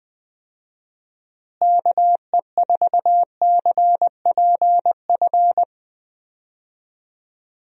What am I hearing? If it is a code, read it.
Morse code, heard as a keyed tone, about 20 wpm, 700 Hz. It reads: KE4CPF
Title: USA Call Signs - 6 Characters - Repeated 2x - 20wpm